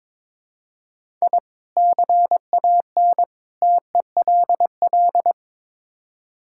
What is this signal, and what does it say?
Morse code, heard as a keyed tone, about 22 wpm, 700 Hz. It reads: I CAN TELL
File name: practice / 22wpm / Build A Phrase - 5-Word Phrases - 22wpm